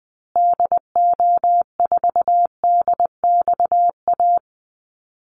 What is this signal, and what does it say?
Morse code, heard as a keyed tone, about 20 wpm, 700 Hz. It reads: DO4DXA
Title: Call Signs Repeated 3x - 20wpm